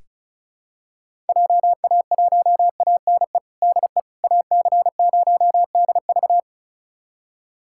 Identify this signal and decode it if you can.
Morse code, heard as a keyed tone, about 34 wpm, 700 Hz. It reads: JA1ANE DE AC0DV
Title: USA Callsign DE Callsign 35wpm